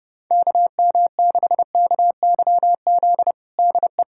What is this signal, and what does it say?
Morse code, heard as a keyed tone, about 30 wpm, 700 Hz. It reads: KM6KYZ DE
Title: USA Callsign DE Callsign 30wpm